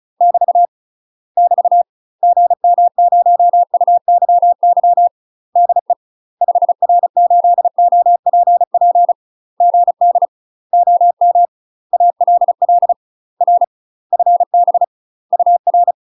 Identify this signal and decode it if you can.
Morse code, heard as a keyed tone, about 34 wpm, 700 Hz. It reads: <BT> <BT> GM0UYY DE 5R8OPP GD OM ALL R FB UR
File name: UK MASTER QSOs TRAINING/35WPM